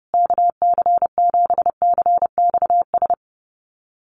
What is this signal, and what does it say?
Morse code, heard as a keyed tone, about 30 wpm, 700 Hz. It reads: KC7CXS
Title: USA Call Signs Repeated 3x 30wpm